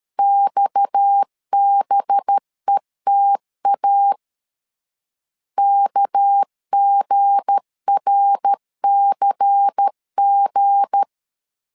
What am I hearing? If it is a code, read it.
Morse code, heard as a keyed tone, about 13 wpm, 800 Hz. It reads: XBETA KGRCG
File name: CW / DET / 1 9 1500C S B 12wpm 3819